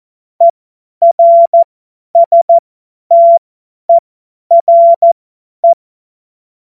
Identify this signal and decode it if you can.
Morse code, heard as a keyed tone, about 14 wpm, spaced as for 10 wpm, 680 Hz. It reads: ERSTERE
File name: Morsekurs / Witze / Witz 28. Dezember 2021, 50 F70